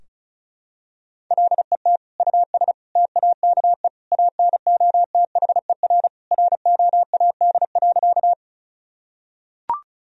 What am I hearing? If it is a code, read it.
Morse code, heard as a keyed tone, about 36 wpm, 700 Hz. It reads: LET US TAKE ANOTHER ROAD.
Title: Sentences from Top 1000 Words - Review Part 2 - 35wpm